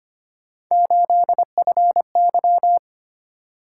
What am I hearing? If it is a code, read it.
Morse code, heard as a keyed tone, about 25 wpm, 700 Hz. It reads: 8FY
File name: Three Letter-Number Combinations - 25wpm